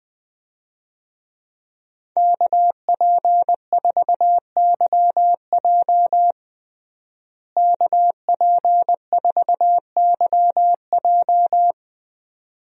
Morse code, heard as a keyed tone, about 20 wpm, 700 Hz. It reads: KP4YJ KP4YJ
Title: DXCC-100 Call Signs - 5 Characters - Repeated 2x - 20wpm